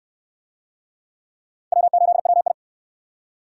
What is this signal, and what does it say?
Morse code, heard as a keyed tone, about 64 wpm, 700 Hz. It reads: W8WI